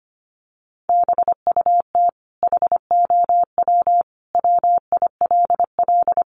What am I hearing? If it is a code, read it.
Morse code, heard as a keyed tone, about 25 wpm, 700 Hz. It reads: BUT HOW WILL
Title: Sentences from Top 1100 Words 25wpm